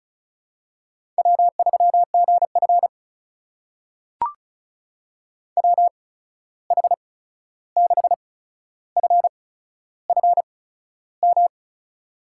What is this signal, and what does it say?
Morse code, heard as a keyed tone, about 36 wpm, spaced as for 13 wpm, 700 Hz. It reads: W3GF WH6FFM
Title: USA Call Signs - 8x Character Spacing - 35wpm